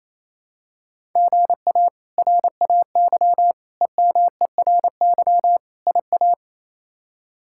Morse code, heard as a keyed tone, about 28 wpm, 700 Hz. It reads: GA RAY EMERY IA
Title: K1USN SST QSO Practice - 28wpm